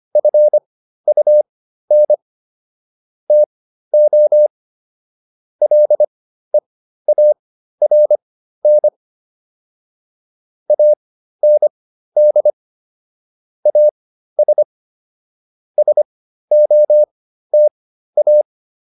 Morse code, heard as a keyed tone, about 25 wpm, spaced as for 13 wpm, 600 Hz. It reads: FUN TO LEARN AND AS SOTA